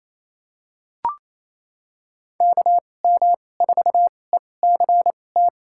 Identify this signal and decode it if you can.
Morse code, heard as a keyed tone, about 28 wpm, spaced as for 20 wpm, 700 Hz. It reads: KM4ECT
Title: USA Call Signs - 6 Characters - 2x Character Spacing - 28wpm